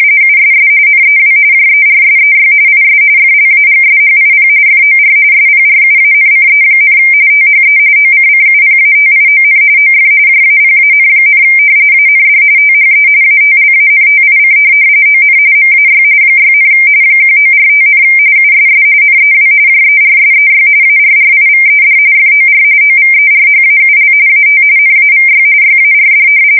RTTY, about 45.45 baud. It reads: USS NEW JERSEY IS FLAGSHIP FOR SPRUANCE, THE OCCUPATION OF JAPAN, AND THE ATLANTIC ON MAY 6, 1945, USS NEW JERSEY ARRIVED FOR OVERHAUL AT NAVY YARD
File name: fsk170 / battleshipnj